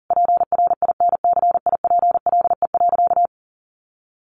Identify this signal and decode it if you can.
Morse code, heard as a keyed tone, about 40 wpm, 700 Hz. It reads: PRINCIPLE.
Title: Sentences from Top 1400 Words 40wpm